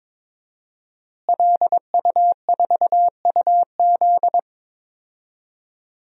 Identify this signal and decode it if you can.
Morse code, heard as a keyed tone, about 22 wpm, 700 Hz. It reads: LU4UZ
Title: DXCC-100 Call Signs - 5 Characters - Repeated 2x - 22wpm